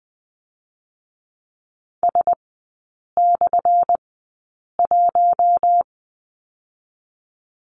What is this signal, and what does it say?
Morse code, heard as a keyed tone, about 20 wpm, spaced as for 8 wpm, 700 Hz. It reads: S/1
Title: 178 Lesson 29e - Slash - Three Characters 20wpm